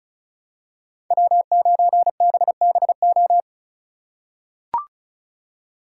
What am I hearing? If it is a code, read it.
Morse code, heard as a keyed tone, about 34 wpm, 700 Hz. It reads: W9BBO
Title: USA Call Signs - 5 Characters - 2x Character Spacing - 35wpm